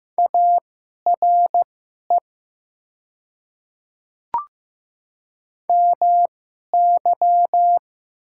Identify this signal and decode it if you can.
Morse code, heard as a keyed tone, about 15 wpm, spaced as for 11 wpm, 700 Hz. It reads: ARE MY
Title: 2x Character Spacing - Top 100 Words - 15wpm